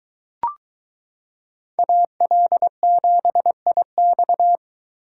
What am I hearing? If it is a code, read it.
Morse code, heard as a keyed tone, about 23 wpm, 700 Hz. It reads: AL7IX